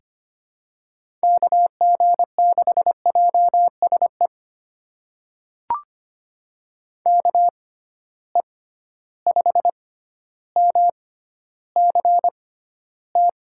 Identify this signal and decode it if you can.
Morse code, heard as a keyed tone, about 25 wpm, spaced as for 13 wpm, 700 Hz. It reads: KG6JSE KE5MCT
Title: USA Call Signs - 6 Characters - 6x Character Spacing - 25wpm